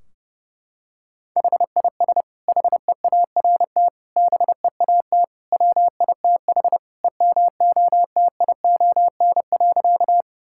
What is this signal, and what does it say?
Morse code, heard as a keyed tone, about 30 wpm, 700 Hz. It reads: HIS HEART BEAT WITH EMOTION.
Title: Sentences from Top 1600 Words 30wpm